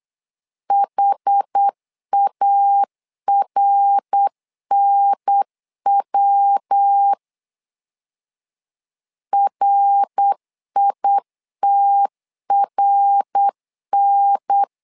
Morse code, heard as a keyed tone, about 8 wpm, 800 Hz. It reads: HARNW RITRN
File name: LEZIONI / 4 Q 8wpm 2732